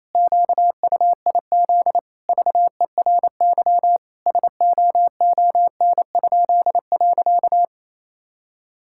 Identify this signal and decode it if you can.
Morse code, heard as a keyed tone, about 28 wpm, 700 Hz. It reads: QUIZ VERY SOON?.
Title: Pangram Challenge v2 - 28wpm